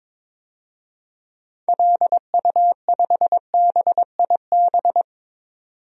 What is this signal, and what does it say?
Morse code, heard as a keyed tone, about 21 wpm, 700 Hz. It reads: LU5BIB
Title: DXCC-100 Call Signs - 6 Characters - 22wpm